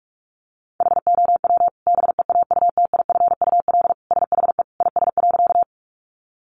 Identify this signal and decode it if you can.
Morse code, heard as a keyed tone, about 44 wpm, 700 Hz. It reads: HOW BEAUTIFUL SHE IS.
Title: Sentences from Top 900 Words 45wpm